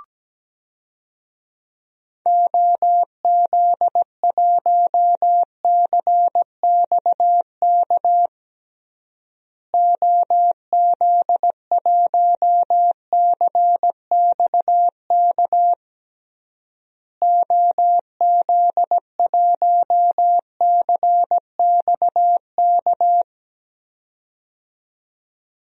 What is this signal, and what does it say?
Morse code, heard as a keyed tone, about 17 wpm, 700 Hz. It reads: OZ1CXK OZ1CXK OZ1CXK
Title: DXCC-100 Call Signs - 6 Characters - Repeated 3x - 17wpm